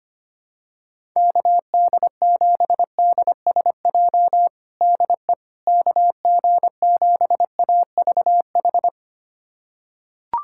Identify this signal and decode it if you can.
Morse code, heard as a keyed tone, about 25 wpm, 700 Hz. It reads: KD7DSJ DE KG7AVH
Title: USA Callsign DE Callsign 25wpm